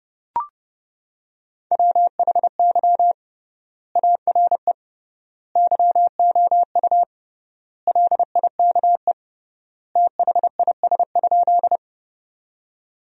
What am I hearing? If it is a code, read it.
Morse code, heard as a keyed tone, about 29 wpm, 700 Hz. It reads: WHY ARE YOU LIKE THIS?
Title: 3x Word-Spacing - Sentences from Top 200 Words - 30wpm